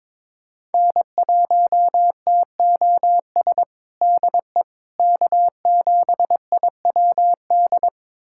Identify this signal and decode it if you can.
Morse code, heard as a keyed tone, about 22 wpm, 700 Hz. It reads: N1TOS DE K7IWD
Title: USA Callsign DE Callsign - 5 Characters - 22wpm